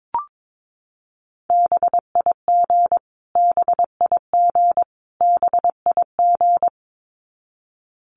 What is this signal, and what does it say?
Morse code, heard as a keyed tone, about 22 wpm, 700 Hz. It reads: BIG BIG BIG
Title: Top 200 Words Repeated 22wpm